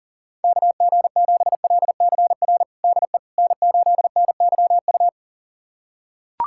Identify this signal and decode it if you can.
Morse code, heard as a keyed tone, about 40 wpm, 700 Hz. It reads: KG7LCR DE N8NYU
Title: USA Callsign DE Callsign 40wpm